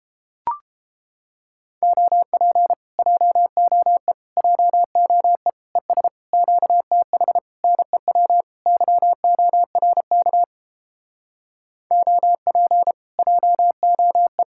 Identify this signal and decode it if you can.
Morse code, heard as a keyed tone, about 30 wpm, 700 Hz. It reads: OP JOE JOE ES QTH NEW YORK OP JOE